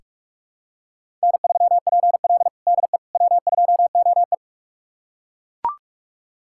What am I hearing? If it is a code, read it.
Morse code, heard as a keyed tone, about 41 wpm, 700 Hz. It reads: N3PL DE W2OE
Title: USA Callsign DE Callsign - 4 Characters - 45wpm